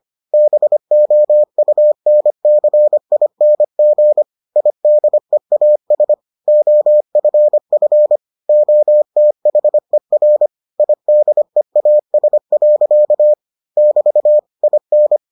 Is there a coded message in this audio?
Yes: Morse code from 0.3 to 15.2 s, heard as a keyed tone, about 25 wpm, 600 Hz. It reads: BOUNCING IDEAS OFF OTHER IDEAS. <BT> IN